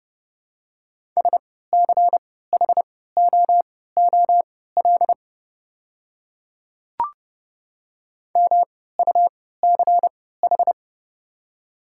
Morse code, heard as a keyed tone, about 30 wpm, spaced as for 17 wpm, 700 Hz. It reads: SCHOOL MUCH
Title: 3x Character Spacing - Top 500 Words - 30wpm